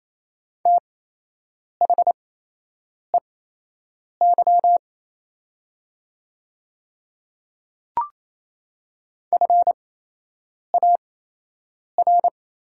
Morse code, heard as a keyed tone, about 28 wpm, spaced as for 8 wpm, 700 Hz. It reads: THEY FAR
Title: 8x Character Spacing - Top 300 Words - 28wpm